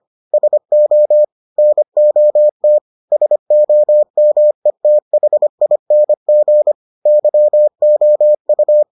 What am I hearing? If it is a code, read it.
Morse code, heard as a keyed tone, about 25 wpm, 600 Hz. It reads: SO NOT SOMETHING YOU